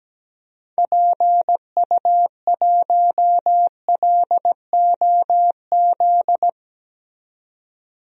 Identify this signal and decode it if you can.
Morse code, heard as a keyed tone, about 17 wpm, 700 Hz. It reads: PU1LOZ
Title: DXCC-100 Call Signs - 6 Characters - 17wpm